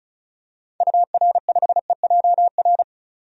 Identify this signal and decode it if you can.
Morse code, heard as a keyed tone, about 36 wpm, 700 Hz. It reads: UR5EJR
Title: DXCC-100 Call Signs - 6 Characters - 35wpm